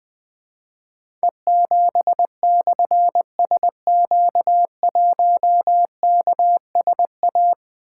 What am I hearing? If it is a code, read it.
Morse code, heard as a keyed tone, about 20 wpm, 700 Hz. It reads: E7/SQ1KSA
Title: Difficult Call Signs 20wpm